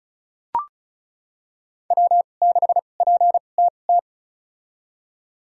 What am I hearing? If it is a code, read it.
Morse code, heard as a keyed tone, about 36 wpm, spaced as for 25 wpm, 700 Hz. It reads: W6PTT
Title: USA Call Signs - 5 Characters - 2x Character Spacing - 35wpm